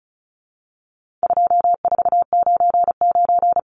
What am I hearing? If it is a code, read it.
Morse code, heard as a keyed tone, about 34 wpm, 700 Hz. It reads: 2499
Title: Numbers - 4-Digit 35wpm